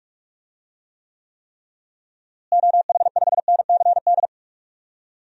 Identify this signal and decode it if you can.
Morse code, heard as a keyed tone, about 46 wpm, 700 Hz. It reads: OH5NCD
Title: DXCC-100 Call Signs - 6 Characters - Repeated 3x - 45wpm